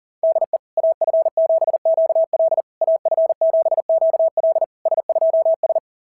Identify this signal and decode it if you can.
Morse code, heard as a keyed tone, about 40 wpm, 690 Hz. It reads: DE AF7QL AF7QL S2S
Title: SOTA QSO Practice 40wpm